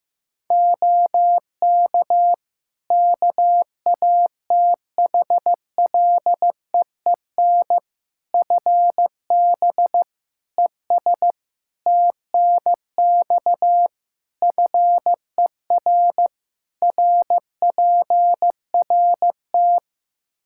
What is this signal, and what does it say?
Morse code, heard as a keyed tone, about 15 wpm, 700 Hz. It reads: OK KATHLEEN FB ES TNX FER RPRT
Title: Practice QSO - Exchange 2 - Info - 15wpm